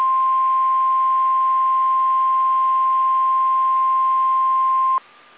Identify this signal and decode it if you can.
Emergency Alert System (SAME) tones (the 1050 Hz attention tone). It is heard from 0.3 to 4.7 s.
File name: eas / KSDZKDJL / KSDZKDJL